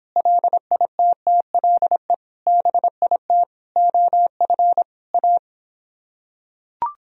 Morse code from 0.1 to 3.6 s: LITTLE BIT

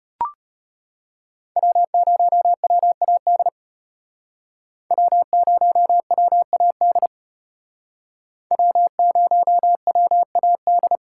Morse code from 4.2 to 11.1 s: W0WAD W0WAD

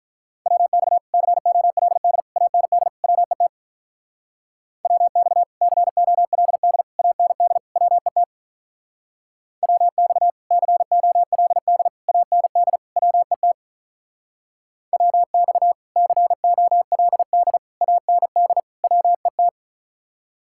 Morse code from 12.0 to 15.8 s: AND WET WX

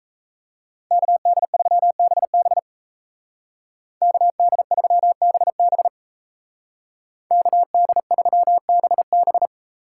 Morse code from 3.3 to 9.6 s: KD3BB KD3BB

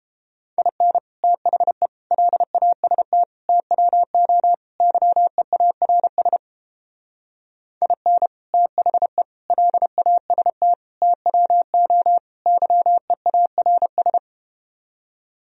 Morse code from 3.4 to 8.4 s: TWO YEARS IN

